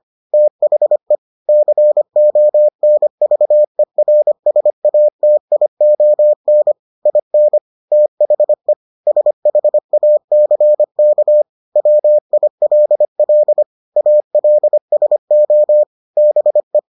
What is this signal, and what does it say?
Morse code, heard as a keyed tone, about 25 wpm, 600 Hz. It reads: THE CONVERSATION IN THE SHACK WILL ALSO BE